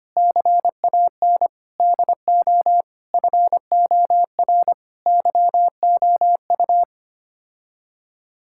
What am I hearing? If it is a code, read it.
Morse code, heard as a keyed tone, about 25 wpm, 700 Hz. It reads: CAN DO FOR YOU
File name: Sets of 9 Words - 25wpm